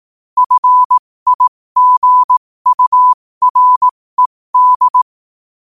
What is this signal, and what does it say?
Morse code, heard as a keyed tone, about 18 wpm, spaced as for 15 wpm, 1000 Hz. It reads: FIGURED
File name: qotdfiles / 15wpm